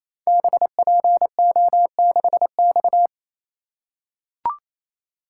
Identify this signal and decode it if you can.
Morse code, heard as a keyed tone, about 29 wpm, 700 Hz. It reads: BPO6X
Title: Code Groups - 5 Letter:Numbers - 28wpm